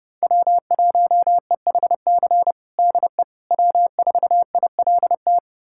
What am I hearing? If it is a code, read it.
Morse code, heard as a keyed tone, about 29 wpm, 700 Hz. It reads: W1EHC DE W4ILT
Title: USA Callsign DE Callsign - 5 Characters - 30wpm